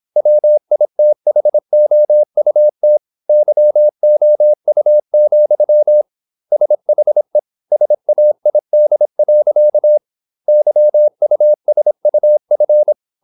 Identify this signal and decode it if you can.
Morse code, heard as a keyed tone, about 26 wpm, 600 Hz. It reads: WITHOUT YOU, SHE SAID. YUSUF
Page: Never 26wpm | CW med Gnister